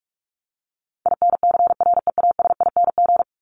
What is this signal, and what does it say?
Morse code, heard as a keyed tone, about 44 wpm, 700 Hz. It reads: INCREASING
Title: ING Suffix - Top 100 Words with Base Words 45wpm